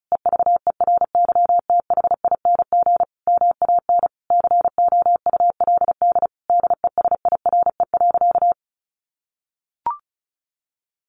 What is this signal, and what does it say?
Morse code, heard as a keyed tone, about 34 wpm, 700 Hz. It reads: EVERYTHING MAN COULD DESIRE.
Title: Sentences from Top 1100 Words 35wpm